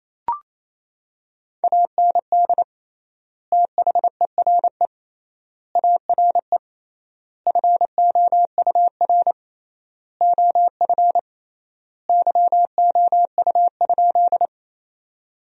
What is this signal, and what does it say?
Morse code, heard as a keyed tone, about 28 wpm, 700 Hz. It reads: AND THERE ARE FOUR OF YOU?
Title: 3x Word-Spacing - Sentences from Top 300 Words - 28wpm